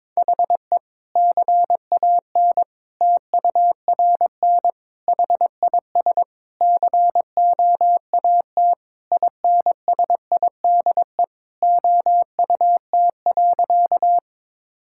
Morse code, heard as a keyed tone, about 21 wpm, 700 Hz. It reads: HE CAN TURN HIS COAT INSIDE OUT.
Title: Sentences from Top 1600 Words 22wpm